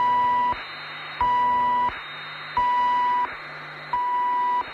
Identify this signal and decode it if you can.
Emergency Alert System (SAME) tones, heard as harsh buzzing bursts. It is heard from 0.6 to 3.8 s.
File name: wrct-log hi- 07